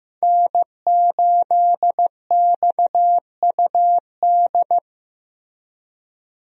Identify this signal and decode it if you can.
Morse code, heard as a keyed tone, about 15 wpm, 700 Hz. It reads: N8XUD
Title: USA Call Signs - 5 Characters - Repeated 2x - 15wpm